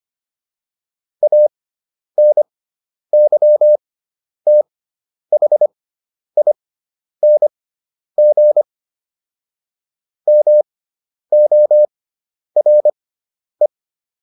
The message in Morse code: ANYTHING MORE

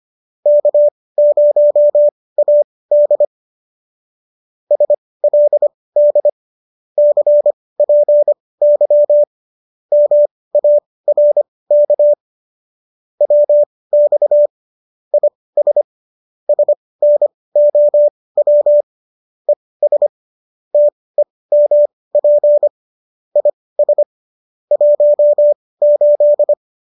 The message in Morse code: K0AD SLD CPY MARK WX IS SNOW ES TEMP IS 18

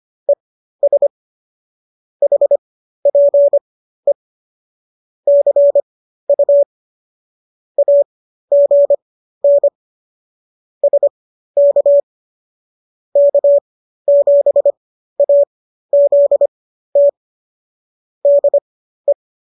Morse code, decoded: ES HPE CU AGN SK K7AZT DE

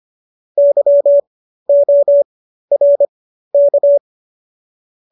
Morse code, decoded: YORK

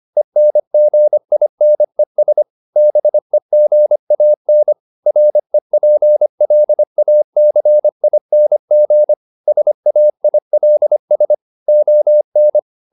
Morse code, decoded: ENGINES BEGAN REPLACING SAILS ON

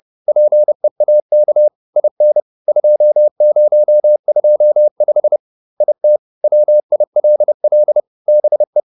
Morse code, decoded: PEAK IN 2025 IT WILL BE

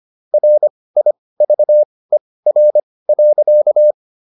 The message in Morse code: RIVER.

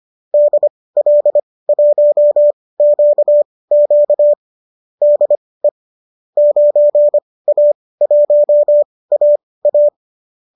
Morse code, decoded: DL1QQ DE 9A1AA

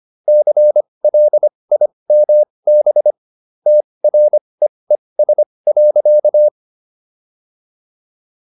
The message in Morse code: CLIMB TREES.

